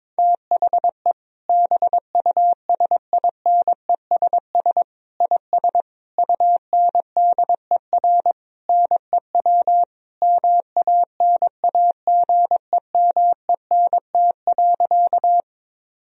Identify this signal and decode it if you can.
Morse code, heard as a keyed tone, about 22 wpm, 700 Hz. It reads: THE BUSINESS IS UNDER NEW MANAGEMENT.
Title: Sentences from Top 1000 Words - Review Part 4 - 22wpm